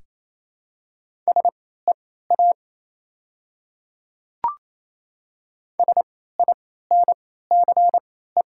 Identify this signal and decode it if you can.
Morse code, heard as a keyed tone, about 28 wpm, spaced as for 16 wpm, 700 Hz. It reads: SEA SINCE
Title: 3x Character Spacing - Top 300 Words - 28wpm